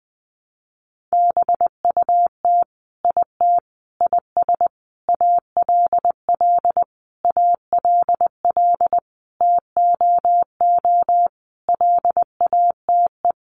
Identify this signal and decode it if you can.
Morse code, heard as a keyed tone, about 20 wpm, 700 Hz. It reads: BUT IT IS ALL ALL TOO LATE
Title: Sentences from Top 300 Words 20wpm